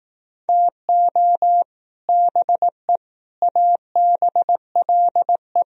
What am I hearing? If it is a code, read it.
Morse code, heard as a keyed tone, about 19 wpm, 700 Hz. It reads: TO BE ABLE
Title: Speed-Racing - Sets of 3 Words 15wpm